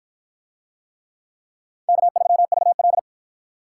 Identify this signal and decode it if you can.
Morse code, heard as a keyed tone, about 53 wpm, 700 Hz. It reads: K3VL